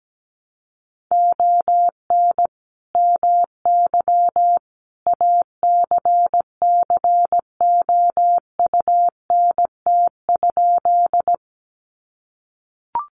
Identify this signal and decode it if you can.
Morse code, heard as a keyed tone, about 17 wpm, 700 Hz. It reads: ON MY ACCOUNT?
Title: Sentences from Top 900 Words 17wpm